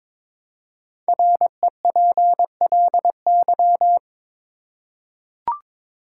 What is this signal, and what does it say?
Morse code, heard as a keyed tone, about 22 wpm, 690 Hz. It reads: REPLY
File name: 5 Letter Words - First 3 Letters Given - 22wpm